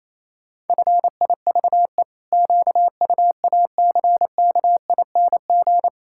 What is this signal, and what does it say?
Morse code, heard as a keyed tone, about 28 wpm, 700 Hz. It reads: FIVE QUACKING
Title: Pangram Challenge v2 - 28wpm